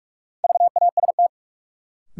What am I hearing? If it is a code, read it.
Morse code, heard as a keyed tone, about 44 wpm, 700 Hz. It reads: VAST
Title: Sentences from Top 1300 Words - New Word Drill - 45wpm